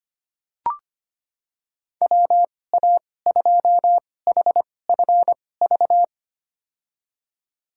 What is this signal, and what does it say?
Morse code, heard as a keyed tone, about 25 wpm, spaced as for 18 wpm, 700 Hz. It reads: WA2HFV
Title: USA Call Signs - 2x Character Spacing - 25wpm